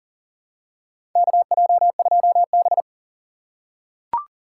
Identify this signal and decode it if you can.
Morse code, heard as a keyed tone, about 40 wpm, 700 Hz. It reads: KJ2B
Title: USA Call Signs - 4 Characters - 6x Character Spacing - 40wpm